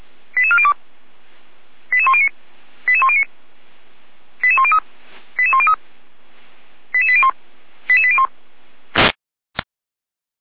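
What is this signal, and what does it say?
Five-tone selective call (ZVEI), heard as quick runs of five tones, tones of about 70 ms. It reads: 88402 88109 88109 88103 88103 88801 88801
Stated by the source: ZVEI five-tone selective call received on SDR, from the feuerwehr-orlamuende.org fire brigade audio server.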